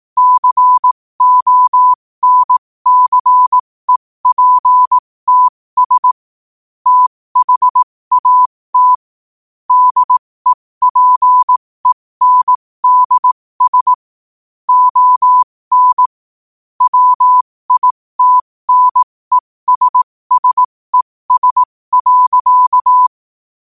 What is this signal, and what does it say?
Morse code, heard as a keyed tone, about 18 wpm, spaced as for 15 wpm, 1000 Hz. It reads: CONCEPTS THAT DEPENDS ON WITNESSES.
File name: qotdfiles / 15wpm